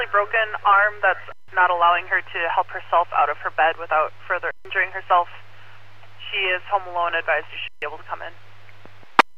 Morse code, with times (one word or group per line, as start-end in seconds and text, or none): none